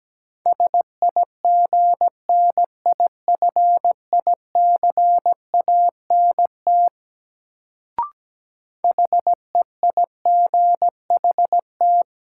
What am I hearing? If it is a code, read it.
Morse code, heard as a keyed tone, about 17 wpm, 700 Hz. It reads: SIGNIFICANT HEIGHT
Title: Sentences from Top 1600 Words - New Word Drill Alt - 17wpm